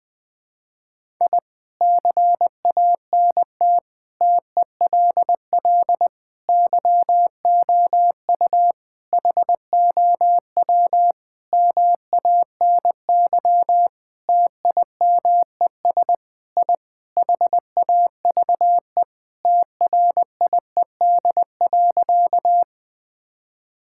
Morse code, heard as a keyed tone, about 20 wpm, 700 Hz. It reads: I CANT TELL YOU HOW MANY TIMES I HAVE TRIED.
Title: Sentences with 6-Word Phrases - 20wpm